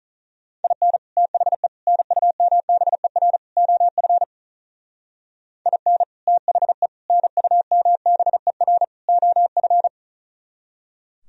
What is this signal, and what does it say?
Morse code, heard as a keyed tone, about 38 wpm, 700 Hz. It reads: IN THE NUMBER OF IN THE NUMBER OF
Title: Speed-Racing - Sets of 4 Words 35wpm